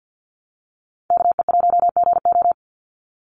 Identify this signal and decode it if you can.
Morse code, heard as a keyed tone, about 50 wpm, 700 Hz. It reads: KE1GO
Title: USA Call Signs 50wpm